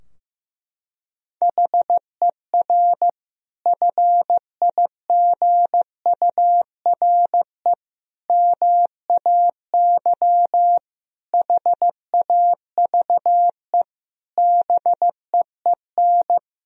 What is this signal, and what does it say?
Morse code, heard as a keyed tone, about 15 wpm, 700 Hz. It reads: HER FIGURE MAY HAVE BEEN
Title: Sentences from Top 1000 Words - Review Part 2 - 15wpm